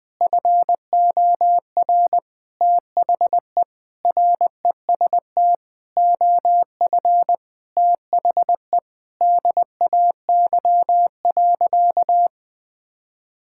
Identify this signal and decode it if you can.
Morse code, heard as a keyed tone, about 20 wpm, 700 Hz. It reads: FOR THE REST OF THE DAY.
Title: Sentences with 5-Word Phrases - 20wpm